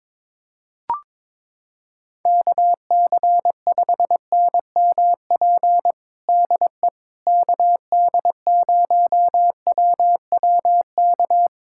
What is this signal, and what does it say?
Morse code, heard as a keyed tone, about 22 wpm, 700 Hz. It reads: KC5NMP DE KD0WWK
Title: USA Callsign DE Callsign 22wpm